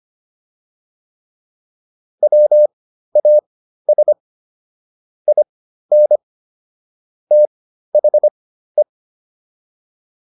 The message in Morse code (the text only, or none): WAS IN THE